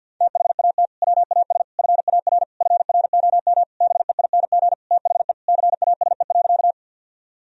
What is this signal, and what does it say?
Morse code, heard as a keyed tone, about 48 wpm, 700 Hz. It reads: THAT WAS FAR FROM BEING THE CASE.
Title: Sentences from Top 1000 Words - Review Part 3 - 50wpm